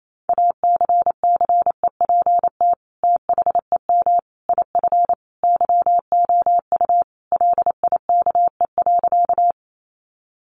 Morse code, heard as a keyed tone, about 28 wpm, 700 Hz. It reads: ACCEPT THEM IF YOU LIKE.
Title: Sentences from Top 1300 Words 28wpm